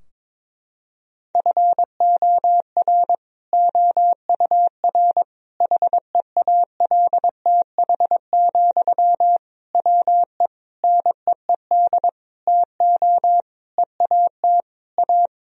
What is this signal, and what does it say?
Morse code, heard as a keyed tone, about 22 wpm, 700 Hz. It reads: FOR OUR HEALTH, WE NEED TO EAT A
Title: Sentences from Top 1000 Words - Review Part 4 - 22wpm